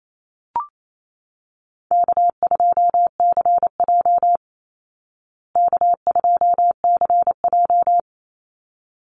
Morse code, heard as a keyed tone, about 28 wpm, 700 Hz. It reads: K2CJ K2CJ
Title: Easy Call Signs Repeated 2x - 28wpm